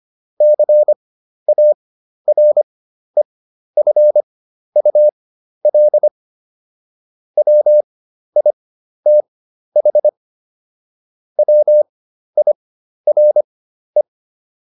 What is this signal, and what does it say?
Morse code, heard as a keyed tone, about 25 wpm, spaced as for 12 wpm, 600 Hz. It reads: CAREFUL WITH WIRE